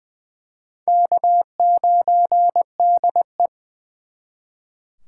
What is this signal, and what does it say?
Morse code, heard as a keyed tone, about 24 wpm, 700 Hz. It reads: K9DE
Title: Speed Racing - USA 1x2 Call Signs 20wpm